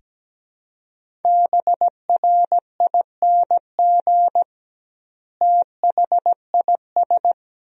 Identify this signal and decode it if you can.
Morse code, heard as a keyed tone, about 17 wpm, 700 Hz. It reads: BRING THIS
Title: 2x Word-Spacing - Sentences from Top 500 Words - 17wpm